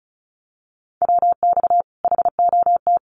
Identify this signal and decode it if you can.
Morse code, heard as a keyed tone, about 34 wpm, 700 Hz. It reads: WX HOT
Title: Common QSO Phrases 35wpm